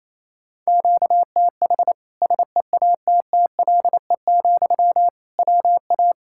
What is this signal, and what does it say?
Morse code, heard as a keyed tone, about 28 wpm, 700 Hz. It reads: QTH SEATTLE, WA
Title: Practice QSO - Exchange 1 - Beginning - 28wpm